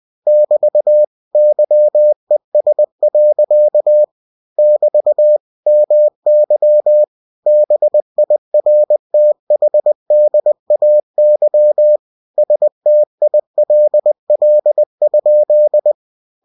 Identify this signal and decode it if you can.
Morse code, heard as a keyed tone, about 20 wpm, 600 Hz. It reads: <BT> YES. <BT> MY BIRTHDAY STILL?